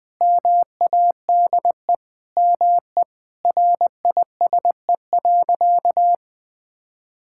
Morse code, heard as a keyed tone, about 20 wpm, 700 Hz. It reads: MADE ME RISE.
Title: Sentences from Top 1000 Words - Review Part 6 - 20wpm